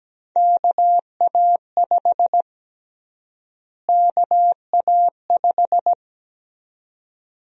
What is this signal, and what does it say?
Morse code, heard as a keyed tone, about 17 wpm, 700 Hz. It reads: KA5 KA5
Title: USA Call Sign Prefixes - Repeated 2x - 17wpm